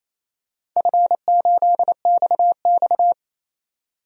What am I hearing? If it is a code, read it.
Morse code, heard as a keyed tone, about 28 wpm, 700 Hz. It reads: F8XX
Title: DXCC-100 Call Signs - 4 Characters - 28wpm